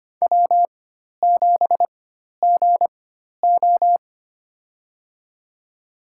Morse code, heard as a keyed tone, about 25 wpm, spaced as for 12 wpm, 700 Hz. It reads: W7GO
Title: USA Call Signs - 4 Characters - 4x Character Spacing - 25wpm